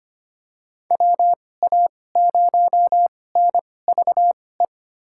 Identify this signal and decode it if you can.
Morse code, heard as a keyed tone, about 25 wpm, spaced as for 18 wpm, 700 Hz. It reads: WA0NVE
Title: USA Call Signs - 6 Characters - 2x Character Spacing - 25wpm